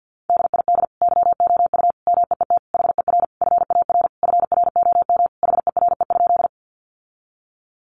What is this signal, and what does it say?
Morse code, heard as a keyed tone, about 48 wpm, 700 Hz. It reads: DID YOU MEET HER FAR FROM HERE?
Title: Sentences from Top 1100 Words 50wpm